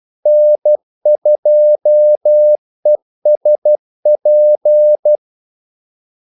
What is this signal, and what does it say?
Morse code, heard as a keyed tone, about 12 wpm, 600 Hz. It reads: N2ESP
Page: 2 ekstra ordmellemrum mellem kaldesignalerne